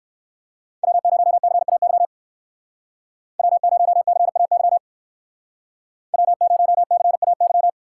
Morse code, heard as a keyed tone, about 58 wpm, 700 Hz. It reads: W0CAY W0CAY W0CAY